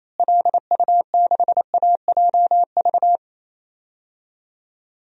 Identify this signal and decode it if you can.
Morse code, heard as a keyed tone, about 28 wpm, 700 Hz. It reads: LU6AJV
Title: DXCC-100 Call Signs - 6 Characters - Repeated 2x - 28wpm